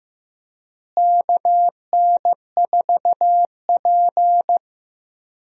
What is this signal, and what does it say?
Morse code, heard as a keyed tone, about 15 wpm, 700 Hz. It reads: KN4P
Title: USA Call Signs - 4 Characters - Repeated 3x - 15wpm